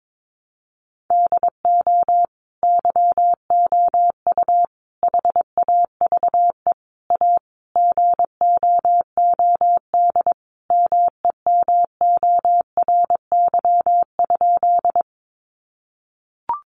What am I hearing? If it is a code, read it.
Morse code, heard as a keyed tone, about 22 wpm, 700 Hz. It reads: DO YOU HAVE A GOOD MEMORY?
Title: Sentences from Top 1300 Words 22wpm